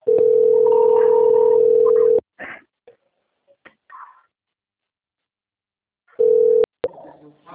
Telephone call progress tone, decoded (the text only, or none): ringback tone